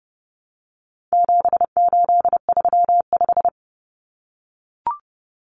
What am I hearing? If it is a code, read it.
Morse code, heard as a keyed tone, about 30 wpm, 700 Hz. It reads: 7835
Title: Numbers - 4-Digit 30wpm